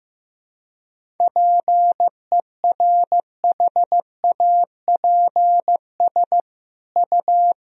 Morse code, heard as a keyed tone, about 15 wpm, 700 Hz. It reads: PERHAPS U
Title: Sentences from Top 500 Words and Abbreviations 15wpm